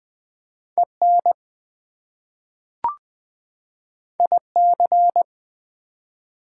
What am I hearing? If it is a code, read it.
Morse code, heard as a keyed tone, about 20 wpm, 700 Hz. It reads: EN IC
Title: Prefixes and Suffixes 20wpm